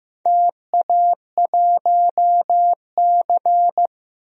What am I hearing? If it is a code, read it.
Morse code, heard as a keyed tone, about 15 wpm, 700 Hz. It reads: TA1C
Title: DXCC-100 Call Signs - 4 Characters - Repeated 3x - 15wpm